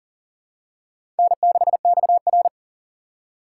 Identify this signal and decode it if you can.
Morse code, heard as a keyed tone, about 40 wpm, 700 Hz. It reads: N6XR
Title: USA Call Signs - 4 Characters - 4x Character Spacing - 40wpm